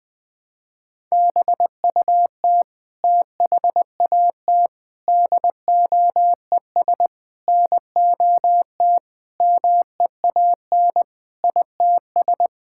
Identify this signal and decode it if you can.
Morse code, heard as a keyed tone, about 21 wpm, 700 Hz. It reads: BUT THAT DOES NOT MEAN ITS
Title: Sentences with 5-Word Phrases - 20wpm